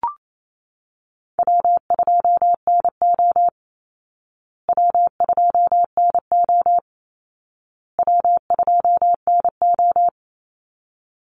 Morse code, heard as a keyed tone, about 28 wpm, 700 Hz. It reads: W2NO W2NO W2NO
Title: Easy Call Signs Repeated 3x - 28wpm